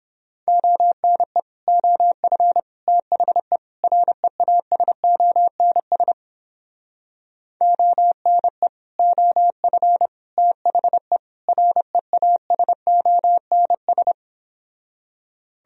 Morse code, heard as a keyed tone, about 31 wpm, 700 Hz. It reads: ONE OF THE REASONS ONE OF THE REASONS